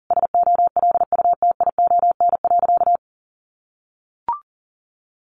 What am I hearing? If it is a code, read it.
Morse code, heard as a keyed tone, about 40 wpm, 700 Hz. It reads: SOLUTION.
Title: Sentences from Top 1300 Words 40wpm